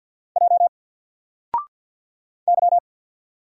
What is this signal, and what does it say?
Morse code, heard as a keyed tone, about 50 wpm, 700 Hz. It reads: J Y
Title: Single Letter-Number 50wpm